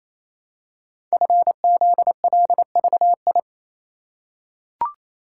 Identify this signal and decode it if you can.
Morse code, heard as a keyed tone, about 28 wpm, 700 Hz. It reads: FZLVI
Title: Code Groups - 5 Letters - 28wpm